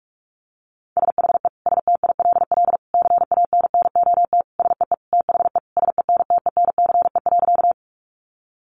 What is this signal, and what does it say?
Morse code, heard as a keyed tone, about 44 wpm, 700 Hz. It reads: SHE STILL CANNOT SEE THE SENTENCE.
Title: Additional Easy Phrases 45wpm